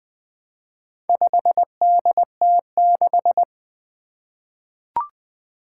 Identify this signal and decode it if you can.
Morse code, heard as a keyed tone, about 20 wpm, 700 Hz. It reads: 5DT6
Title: Four Challenging Letter-Number Combinations 20wpm